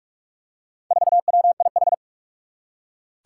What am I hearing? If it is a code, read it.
Morse code, heard as a keyed tone, about 44 wpm, 700 Hz. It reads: 4WIH
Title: Four Challenging Letter-Number Combinations 45wpm